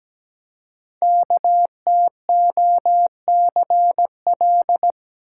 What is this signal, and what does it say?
Morse code, heard as a keyed tone, about 17 wpm, 700 Hz. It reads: KTOCL
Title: Code Groups - 5 Letters - 17wpm